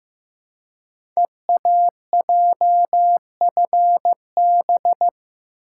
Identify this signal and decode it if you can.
Morse code, heard as a keyed tone, about 15 wpm, 700 Hz. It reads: EAJFB
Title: Code Groups - 5 Letter:Numbers - 15wpm